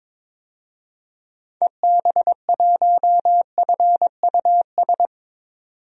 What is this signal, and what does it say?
Morse code, heard as a keyed tone, about 22 wpm, 700 Hz. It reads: EB1FUS
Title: DXCC-100 Call Signs - 6 Characters - 22wpm